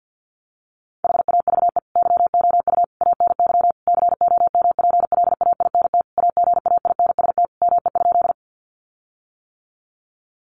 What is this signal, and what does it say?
Morse code, heard as a keyed tone, about 50 wpm, 700 Hz. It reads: HAVE YOU ANY COMPLAINT AGAINST ME?
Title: Sentences from Top 1400 Words 50wpm